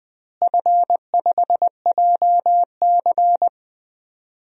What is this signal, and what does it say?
Morse code, heard as a keyed tone, about 20 wpm, 700 Hz. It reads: F5JC
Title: DXCC-100 Call Signs - 4 Characters - Repeated 3x - 20wpm